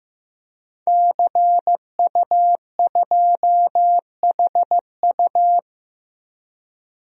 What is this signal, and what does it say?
Morse code, heard as a keyed tone, about 15 wpm, 700 Hz. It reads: CU2HU
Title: DXCC-100 Call Signs - 5 Characters - 15wpm